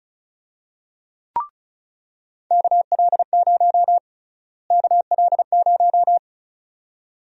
Morse code, heard as a keyed tone, about 34 wpm, 700 Hz. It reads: KL0 KL0
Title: USA Call Sign Prefixes - Repeated 2x - 35wpm